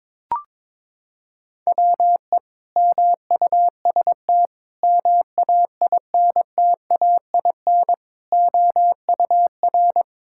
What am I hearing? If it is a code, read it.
Morse code, heard as a keyed tone, about 22 wpm, 700 Hz. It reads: WE MUST MAINTAIN OUR
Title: Sentences from Top 1000 Words - Review Part 5 - 22wpm